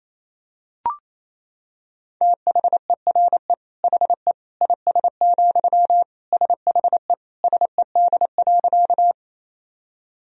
Morse code, heard as a keyed tone, about 28 wpm, 700 Hz. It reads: THERE HE IS, SHE SED.
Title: Sentences from Top 100 Words and Abbreviations 28wpm